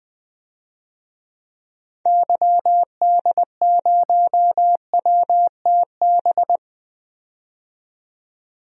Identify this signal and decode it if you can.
Morse code, heard as a keyed tone, about 21 wpm, 700 Hz. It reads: YD0WTB
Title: DXCC-100 Call Signs - 6 Characters - 20wpm